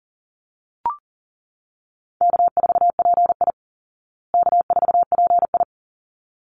Morse code, heard as a keyed tone, about 40 wpm, 700 Hz. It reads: K4PI K4PI
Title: Easy Call Signs Repeated 2x - 40wpm